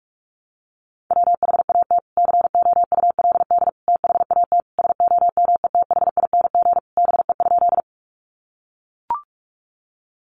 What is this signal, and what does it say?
Morse code, heard as a keyed tone, about 44 wpm, 700 Hz. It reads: WHAT COULD THAT SOMETHING BE?
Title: Sentences from Top 600 Words 45wpm